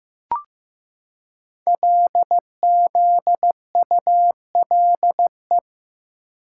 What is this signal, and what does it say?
Morse code, heard as a keyed tone, about 15 wpm, 700 Hz. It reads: LZULE
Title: Code Groups - 5 Letters - 15wpm